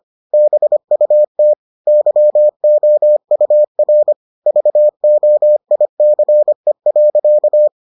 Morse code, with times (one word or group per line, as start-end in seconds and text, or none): 0.3-1.5 BUT
1.9-4.1 YOUR
4.5-7.7 VOICE.